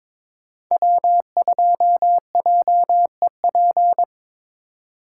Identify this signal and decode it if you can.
Morse code, heard as a keyed tone, about 26 wpm, 700 Hz. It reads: W2JEP
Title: Speed Racing - USA 1x3 Call Signs 22wpm